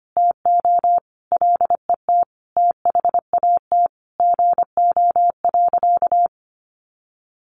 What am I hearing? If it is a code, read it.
Morse code, heard as a keyed tone, about 25 wpm, 700 Hz. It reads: TO LET THAT GO.
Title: Sentences from Top 300 Words 25wpm